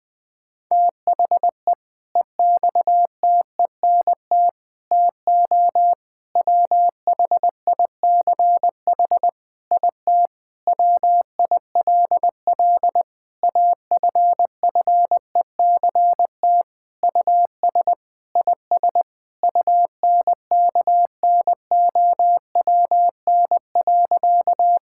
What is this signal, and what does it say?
Morse code, heard as a keyed tone, about 20 wpm, 700 Hz. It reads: THE EXTENT TO WHICH IT WILL AFFECT US IS UNKNOWN.
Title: Sentences with 4-Word Phrases - Spoken Beginning - 20wpm